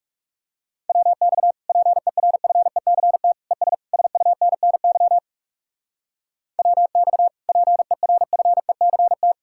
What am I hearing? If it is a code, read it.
Morse code, heard as a keyed tone, about 37 wpm, 700 Hz. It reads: WX PERFECT ES SUNNY WX PERFECT
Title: Speed-Racing Common QSO Phrases - 30wpm